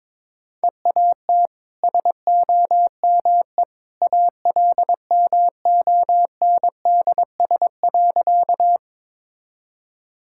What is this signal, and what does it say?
Morse code, heard as a keyed tone, about 22 wpm, 700 Hz. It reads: EAT SOME ALMONDS.
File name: practice / 22wpm / Easy Sentences with Hints - 22wpm